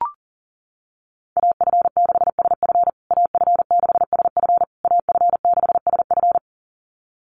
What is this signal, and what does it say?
Morse code, heard as a keyed tone, about 40 wpm, 700 Hz. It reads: AF6SF AF6SF AF6SF
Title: USA Call Signs Repeated 3x 40wpm